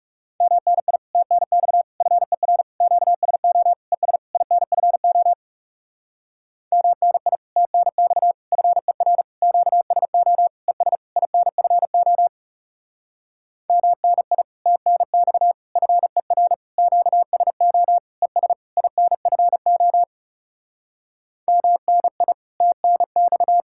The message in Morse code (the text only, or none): MNI TNX FER QSO ES INFO MNI TNX FER QSO ES INFO MNI TNX FER QSO ES INFO MNI TNX